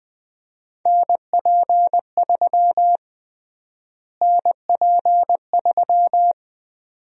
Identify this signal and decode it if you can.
Morse code, heard as a keyed tone, about 21 wpm, 700 Hz. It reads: NP3 NP3
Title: USA Call Sign Prefixes - Repeated 2x - 20wpm